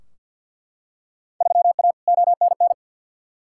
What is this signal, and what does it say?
Morse code, heard as a keyed tone, about 50 wpm, 700 Hz. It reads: 3A ONN
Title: Field Day Exchanges Encore - 50wpm